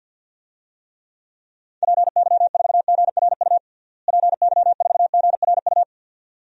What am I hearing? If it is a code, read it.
Morse code, heard as a keyed tone, about 50 wpm, 700 Hz. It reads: PY4GRU PY4GRU
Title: DXCC-100 Call Signs - 6 Characters - Repeated 2x - 50wpm